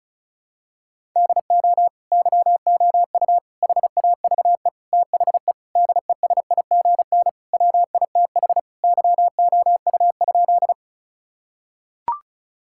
Morse code, heard as a keyed tone, about 36 wpm, 700 Hz. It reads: DO YOU HAVE THE DESIGN WITH YOU?
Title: Sentences from Top 1000 Words - Review Part 5 - 35wpm